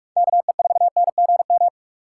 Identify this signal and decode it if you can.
Morse code, heard as a keyed tone, about 46 wpm, 700 Hz. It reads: KE4NGM
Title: USA Call Signs - 6 Characters - 6x Character Spacing - 45wpm